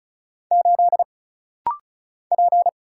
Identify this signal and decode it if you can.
Morse code, heard as a keyed tone, about 34 wpm, 700 Hz. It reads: 8 P
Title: Single Letter-Number Alt - Rapid-Fire 35wpm